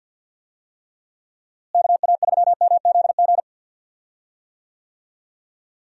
Morse code, heard as a keyed tone, about 53 wpm, 700 Hz. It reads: KA3MZG
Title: USA Call Signs - 6 Characters - Repeated 3x - 50wpm